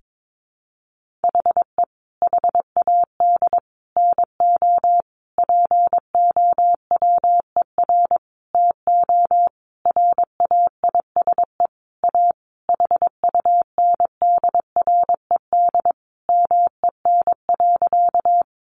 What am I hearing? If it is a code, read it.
Morse code, heard as a keyed tone, about 22 wpm, 700 Hz. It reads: HE HAD NO POWER TO RAISE A HUNDRED MEN.
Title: Sentences from Top 1400 Words 22wpm